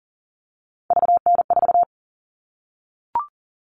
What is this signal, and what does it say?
Morse code, heard as a keyed tone, about 40 wpm, 700 Hz. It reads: VN4
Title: All Cut Numbers - 3-Digit 40wpm